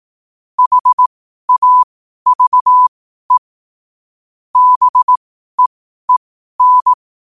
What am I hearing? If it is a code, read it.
Morse code, heard as a keyed tone, about 18 wpm, spaced as for 12 wpm, 1000 Hz. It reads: HAVE BEEN
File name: qotdfiles / 12wpm